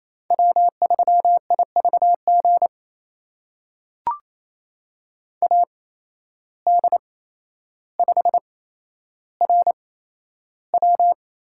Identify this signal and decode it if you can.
Morse code, heard as a keyed tone, about 28 wpm, spaced as for 12 wpm, 700 Hz. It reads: W3IVG AD5RW
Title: USA Call Signs - 5 Characters - 8x Character Spacing - 28wpm